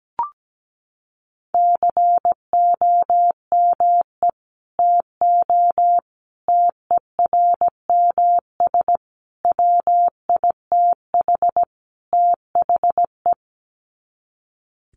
Morse code, heard as a keyed tone, about 17 wpm, 700 Hz. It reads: COME TO TERMS WITH THE
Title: Sets of 5 Words Encore2 17wpm